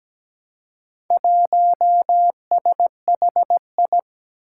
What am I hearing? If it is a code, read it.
Morse code, heard as a keyed tone, about 17 wpm, 700 Hz. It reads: 1SHI
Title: Four Challenging Letter-Number Combinations 17wpm